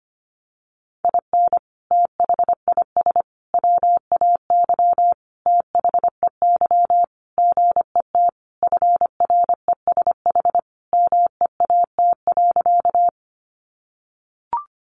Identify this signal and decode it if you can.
Morse code, heard as a keyed tone, about 25 wpm, 700 Hz. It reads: IN THIS WAY THEY GET FRESH MEAT.
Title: Sentences from Top 1200 Words 25wpm